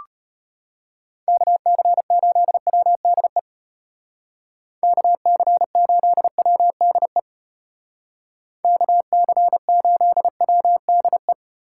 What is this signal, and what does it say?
Morse code, heard as a keyed tone, about 30 wpm, 700 Hz. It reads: KC8WDE KC8WDE KC8WDE
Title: Speed Racing - USA 2x3 Call Signs 25wpm